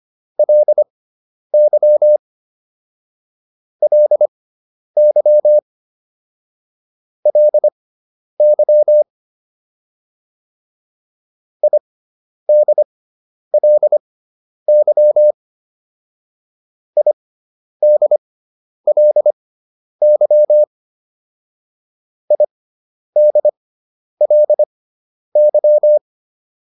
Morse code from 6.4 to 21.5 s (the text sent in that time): LY IDLY IDLY